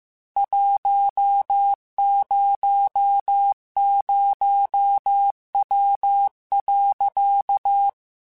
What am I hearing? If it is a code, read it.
Morse code, heard as a keyed tone, about 15 wpm, 800 Hz. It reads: 100W.
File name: QSOs / QSOs-mp3s / WD0FBT 1@13